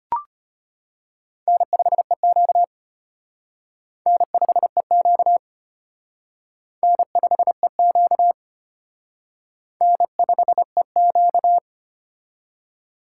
Morse code, heard as a keyed tone, about 33 wpm, 700 Hz. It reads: N5EQ N5EQ N5EQ N5EQ